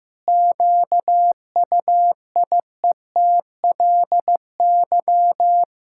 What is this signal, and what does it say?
Morse code, heard as a keyed tone, about 15 wpm, 700 Hz. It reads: QUIETLY
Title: Pangram Challenge v2 - 15wpm